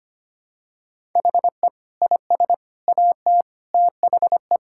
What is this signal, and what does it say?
Morse code, heard as a keyed tone, about 25 wpm, 700 Hz. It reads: HE IS AT THE
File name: practice / 25wpm / Sentences with 4-Word Phrases - 25wpm